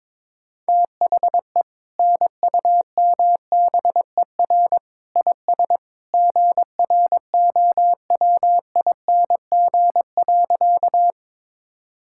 Morse code, heard as a keyed tone, about 21 wpm, 700 Hz. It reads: THE NUMBER IS GROWING.
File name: practice / 22wpm / Sentences with 2-Word Phrases - 22wpm